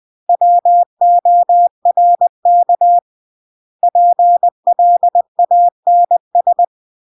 Morse code, heard as a keyed tone, about 20 wpm, 700 Hz. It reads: WORK PLANS
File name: CW / jodorowsky / 82 commandments 20-20